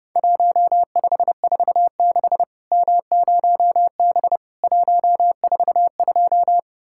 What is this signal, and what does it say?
Morse code, heard as a keyed tone, about 30 wpm, 700 Hz. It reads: 1546 M0B 142
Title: Worked All Europe DX Contest - QTC Exchange - 30wpm